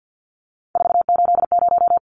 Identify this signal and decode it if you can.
Morse code, heard as a keyed tone, about 53 wpm, 700 Hz. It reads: 480
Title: Numbers - 3-Digit 50wpm